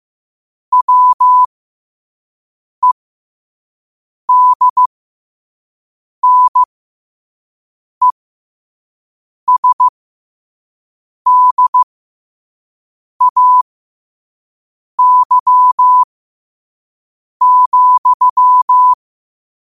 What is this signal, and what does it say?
Morse code, heard as a keyed tone, about 15 wpm, spaced as for 5 wpm, 1000 Hz. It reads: WEDNESDAY,
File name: qotdfiles / 5wpm